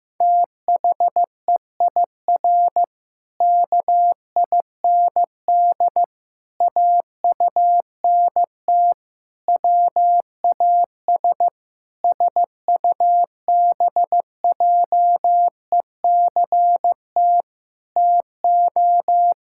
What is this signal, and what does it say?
Morse code, heard as a keyed tone, about 15 wpm, 700 Hz. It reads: THEIR KIND AUNT WAS SUBJECT TO
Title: Pangram Challenge v2 - 15wpm